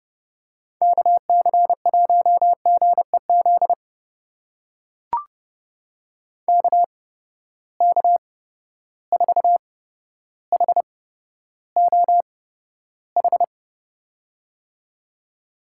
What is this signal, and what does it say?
Morse code, heard as a keyed tone, about 30 wpm, spaced as for 13 wpm, 700 Hz. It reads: KC1GEZ KK4HOH
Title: USA Call Signs - 6 Characters - 8x Character Spacing - 30wpm